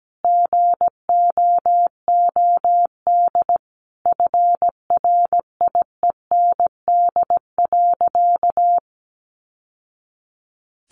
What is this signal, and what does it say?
Morse code, heard as a keyed tone, about 17 wpm, 700 Hz. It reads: GOOD FRIEND.
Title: Sentences from Top 600 Words 17wpm